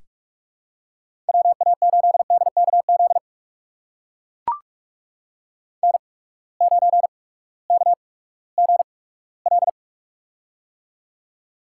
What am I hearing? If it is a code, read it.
Morse code, heard as a keyed tone, about 42 wpm, spaced as for 20 wpm, 700 Hz. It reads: WA8DKZ N9KGL
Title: USA Call Signs - 8x Character Spacing - 45wpm